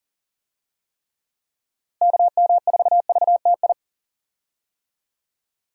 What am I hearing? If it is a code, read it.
Morse code, heard as a keyed tone, about 40 wpm, 700 Hz. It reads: KM4VTI
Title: USA Call Signs - 6 Characters - Repeated 2x - 40wpm